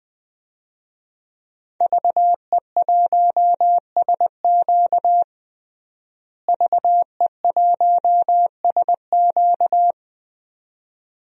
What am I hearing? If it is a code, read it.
Morse code, heard as a keyed tone, about 20 wpm, 700 Hz. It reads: VE1SQ VE1SQ
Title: DXCC-100 Call Signs - 5 Characters - Repeated 2x - 20wpm